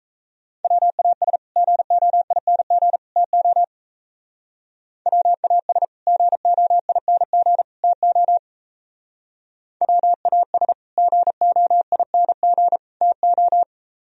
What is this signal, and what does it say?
Morse code, heard as a keyed tone, about 36 wpm, 700 Hz. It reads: WAS GOING TO WAS GOING TO WAS GOING TO